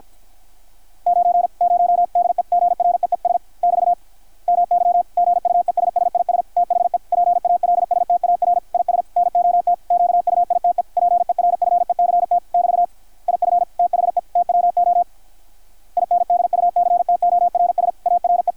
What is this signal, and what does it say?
Morse code, heard as a keyed tone, about 48 wpm, 660 Hz, with radio noise. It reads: 90 DEGREES <BT> MY GUESS IS THE PALSTAR IS NOT QUITE PERFECT <BT> IF THE TWO INDUCTORS ARE
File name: archived / CRSnet